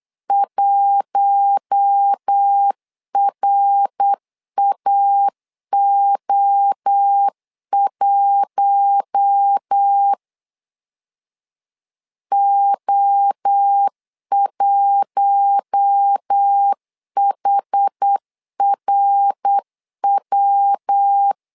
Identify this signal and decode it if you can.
Morse code, heard as a keyed tone, about 8 wpm, 800 Hz. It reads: 1RAO1 O1HRW
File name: LEZIONI / 4 Q1 8wpm 3035